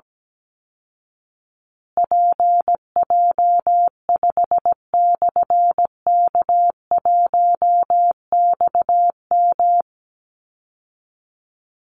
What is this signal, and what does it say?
Morse code, heard as a keyed tone, about 17 wpm, 700 Hz. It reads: PJ5/K1XM
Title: Call Signs Repeated 3x - 17wpm